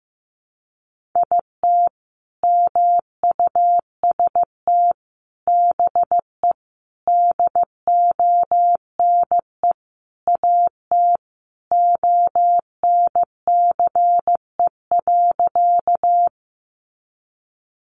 Morse code, heard as a keyed tone, about 15 wpm, 700 Hz. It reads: IT MUST BE DONE AT ONCE.
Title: Sentences from Top 500 Words 15wpm